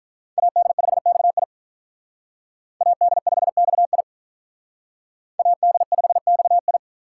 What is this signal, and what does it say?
Morse code, heard as a keyed tone, about 44 wpm, 700 Hz. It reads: AD5XI AD5XI AD5XI